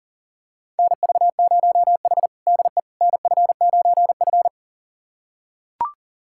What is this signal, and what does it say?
Morse code, heard as a keyed tone, about 40 wpm, 700 Hz. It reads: NV0H DE NF9F
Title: USA Callsign DE Callsign - 4 Characters - 40wpm